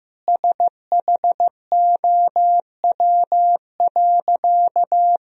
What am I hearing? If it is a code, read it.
Morse code, heard as a keyed tone, about 15 wpm, 700 Hz. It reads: SHOW.
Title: Sentences from Top 1000 Words - Review Part 3 - 15wpm